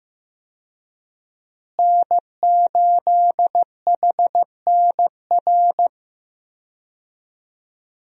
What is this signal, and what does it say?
Morse code, heard as a keyed tone, about 15 wpm, 700 Hz. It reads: N8HNR
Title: USA Call Signs - 5 Characters - Repeated 3x - 15wpm